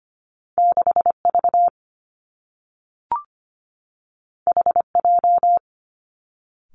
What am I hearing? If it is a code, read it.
Morse code, heard as a keyed tone, about 25 wpm, 700 Hz. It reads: 6V HJ
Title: Two Challenging Letter-Number Combinations 25wpm